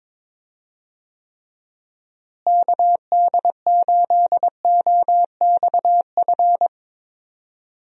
Morse code, heard as a keyed tone, about 22 wpm, 700 Hz. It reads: KD8OXF
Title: USA Call Signs - 6 Characters - Repeated 3x - 22wpm